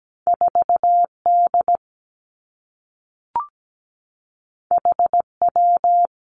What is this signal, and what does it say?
Morse code, heard as a keyed tone, about 17 wpm, 700 Hz. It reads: 4D HW
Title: Two Challenging Letter-Number Combinations 17wpm